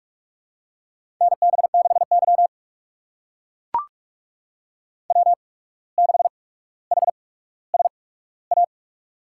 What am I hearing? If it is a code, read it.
Morse code, heard as a keyed tone, about 44 wpm, spaced as for 18 wpm, 700 Hz. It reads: NB6Y W6HSA
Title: USA Call Signs - 8x Character Spacing - 45wpm